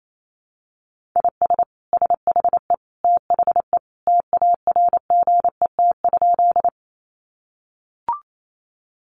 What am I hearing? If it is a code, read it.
Morse code, heard as a keyed tone, about 28 wpm, 700 Hz. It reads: IS SHE THE TARGET?
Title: Sentences from Top 1200 Words 28wpm